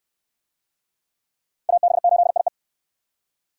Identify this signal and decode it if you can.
Morse code, heard as a keyed tone, about 62 wpm, 700 Hz. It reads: AB8IE